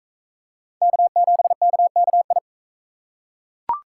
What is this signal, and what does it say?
Morse code, heard as a keyed tone, about 38 wpm, 700 Hz. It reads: K7KKI